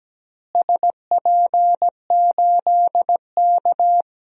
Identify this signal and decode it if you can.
Morse code, heard as a keyed tone, about 17 wpm, 700 Hz. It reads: SP8K
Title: DXCC-100 Call Signs - 4 Characters - Repeated 3x - 17wpm